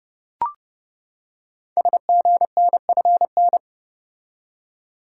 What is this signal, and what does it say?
Morse code, heard as a keyed tone, about 30 wpm, 700 Hz. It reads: SGNFN
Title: Code Groups - 5 Letters - 30wpm